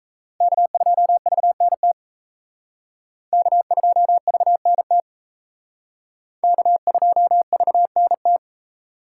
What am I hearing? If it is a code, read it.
Morse code, heard as a keyed tone, about 35 wpm, 700 Hz. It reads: K2VNT K2VNT K2VNT